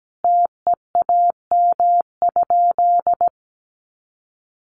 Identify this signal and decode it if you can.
Morse code, heard as a keyed tone, about 17 wpm, 700 Hz. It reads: TEAM?
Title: Sentences from Top 1500 Words 17wpm